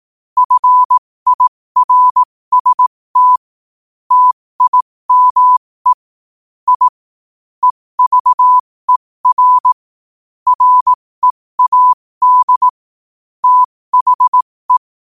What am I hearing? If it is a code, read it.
Morse code, heard as a keyed tone, about 18 wpm, spaced as for 15 wpm, 1000 Hz. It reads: FIRST TIME I EVER READ THE
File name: qotdfiles / 15wpm